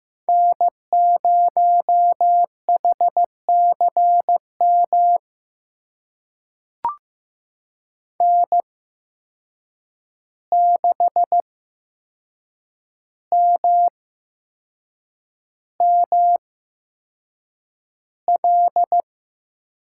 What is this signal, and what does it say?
Morse code, heard as a keyed tone, about 15 wpm, spaced as for 6 wpm, 700 Hz. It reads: N0HCM N6MML
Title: USA Call Signs - 5 Characters - 8x Character Spacing - 15wpm